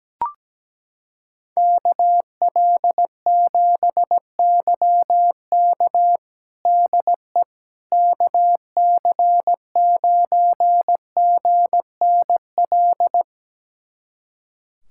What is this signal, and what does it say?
Morse code, heard as a keyed tone, about 17 wpm, 700 Hz. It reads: KL7YK DE KC9GNL
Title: USA Callsign DE Callsign 17wpm